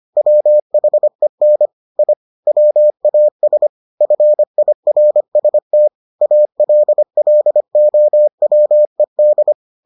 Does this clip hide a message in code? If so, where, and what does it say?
Yes: Morse code from 0.2 to 9.5 s, heard as a keyed tone, about 25 wpm, 600 Hz. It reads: WHEN I WAS FIRST ALLOWED